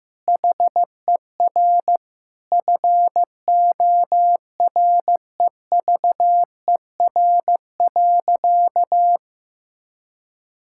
Morse code, heard as a keyed tone, about 15 wpm, 700 Hz. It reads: HER FOREVER.
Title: Sentences from Top 1600 Words 15wpm